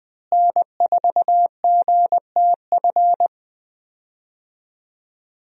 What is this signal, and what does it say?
Morse code, heard as a keyed tone, about 20 wpm, 700 Hz. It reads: N4GTF
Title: USA Call Signs - 5 Characters - Repeated 2x - 20wpm